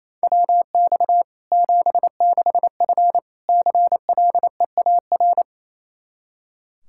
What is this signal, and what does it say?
Morse code, heard as a keyed tone, about 28 wpm, 700 Hz. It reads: WX 76F CLEAR
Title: K1USN SST QSO Practice 28wpm